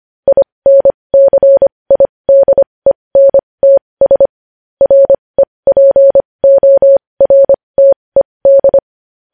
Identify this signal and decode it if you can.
Morse code, heard as a keyed tone, about 25 wpm, spaced as for 20 wpm, 550 Hz. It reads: INCIDENTS REPORTED
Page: news headlines in morse code at 20 wpm